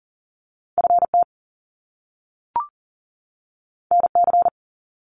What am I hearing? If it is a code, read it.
Morse code, heard as a keyed tone, about 38 wpm, 700 Hz. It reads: FT NC
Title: Top Two Letter Combinations in Words 40wpm